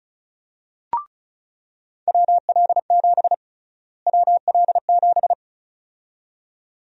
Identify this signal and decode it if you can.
Morse code, heard as a keyed tone, about 36 wpm, 700 Hz. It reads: WL7 WL7
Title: USA Call Sign Prefixes - Repeated 2x - 35wpm